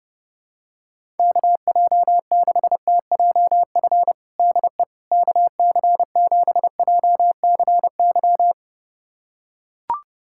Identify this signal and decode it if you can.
Morse code, heard as a keyed tone, about 31 wpm, 700 Hz. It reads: KJ6TJF DE KC7JCY
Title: USA Callsign DE Callsign - 6 Characters - 30wpm